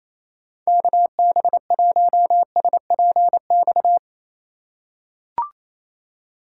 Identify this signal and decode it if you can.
Morse code, heard as a keyed tone, about 28 wpm, 700 Hz. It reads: KB1SPX
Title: USA Call Signs - 6 Characters - 2x Character Spacing - 28wpm